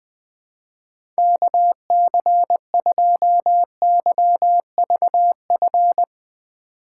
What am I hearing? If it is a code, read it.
Morse code, heard as a keyed tone, about 20 wpm, 700 Hz. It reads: KC2YVF
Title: USA Call Signs - 6 Characters - 3x Character Spacing - 20wpm